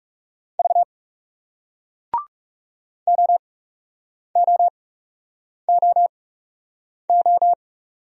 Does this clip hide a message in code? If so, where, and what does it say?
Yes: Morse code from 0.6 to 7.5 s, heard as a keyed tone, about 40 wpm, spaced as for 8 wpm, 700 Hz. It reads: V OOOO